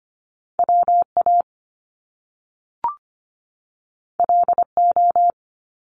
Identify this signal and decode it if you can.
Morse code, heard as a keyed tone, about 25 wpm, 700 Hz. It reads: WA LO
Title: Top Two Letter Combinations in Words 25wpm